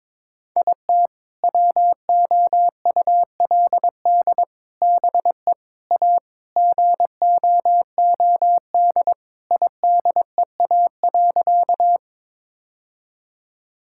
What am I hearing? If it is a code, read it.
Morse code, heard as a keyed tone, about 21 wpm, 700 Hz. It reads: IT WOULD BE A GOOD IDEA.
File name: practice / 22wpm / Sentences with 3-Word Phrases - 22wpm